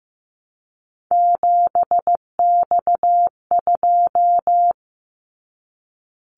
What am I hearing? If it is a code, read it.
Morse code, heard as a keyed tone, about 15 wpm, 700 Hz. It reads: 7X2
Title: Callsign Prefixes for Top 100 DXCC - 15wpm